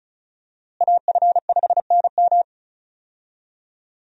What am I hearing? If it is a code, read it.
Morse code, heard as a keyed tone, about 34 wpm, 700 Hz. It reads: AF5NM
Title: USA Call Signs - 5 Characters - Repeated 2x - 35wpm